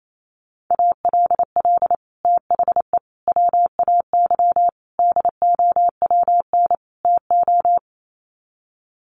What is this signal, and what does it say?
Morse code, heard as a keyed tone, about 28 wpm, 700 Hz. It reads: ALL THE WAY DOWN TO
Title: Sets of 5 Words Encore2 28wpm